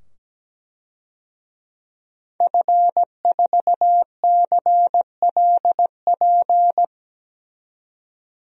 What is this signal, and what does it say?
Morse code, heard as a keyed tone, about 17 wpm, 700 Hz. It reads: F4CLP
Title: DXCC-100 Call Signs - 5 Characters - 17wpm